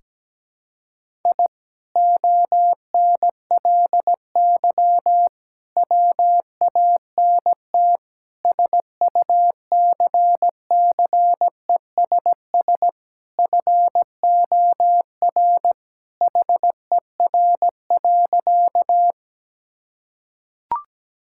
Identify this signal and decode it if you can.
Morse code, heard as a keyed tone, about 17 wpm, 700 Hz. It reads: I ONLY WANT SUCCESS FOR HER.
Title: Sentences from Top 1000 Words - Review Part 6 - 17wpm